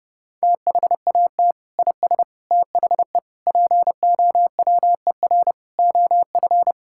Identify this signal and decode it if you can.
Morse code, heard as a keyed tone, about 30 wpm, 700 Hz. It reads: THAT IS THE POWER OF
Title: Sentences from Top 1000 Words - Review Part 5 - 30wpm